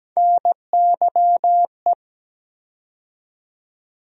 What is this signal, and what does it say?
Morse code, heard as a keyed tone, about 17 wpm, 700 Hz. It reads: NYE
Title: USA Call Sign Postfixes - 3 Letters 17wpm